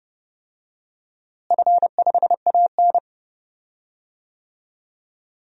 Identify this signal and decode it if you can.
Morse code, heard as a keyed tone, about 30 wpm, 700 Hz. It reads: F5AN
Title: DXCC-100 Call Signs - 4 Characters - Repeated 2x - 30wpm